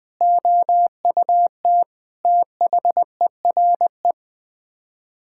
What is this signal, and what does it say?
Morse code, heard as a keyed tone, about 20 wpm, 700 Hz. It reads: OUT THERE
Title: Sets of 8 Words - 20wpm